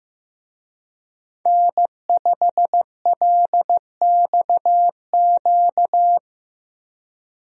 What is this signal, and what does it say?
Morse code, heard as a keyed tone, about 16 wpm, 700 Hz. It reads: N5LXQ
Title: USA Call Signs - 5 Characters - Repeated 2x - 15wpm